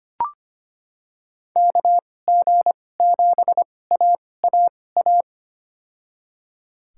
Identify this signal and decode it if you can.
Morse code, heard as a keyed tone, about 25 wpm, spaced as for 18 wpm, 700 Hz. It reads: KG7AAA
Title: USA Call Signs - 6 Characters - 2x Character Spacing - 25wpm